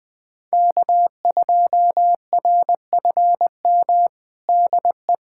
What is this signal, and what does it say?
Morse code, heard as a keyed tone, about 20 wpm, 700 Hz. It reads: K2RFM DE
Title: USA Callsign DE Callsign - 5 Characters - 20wpm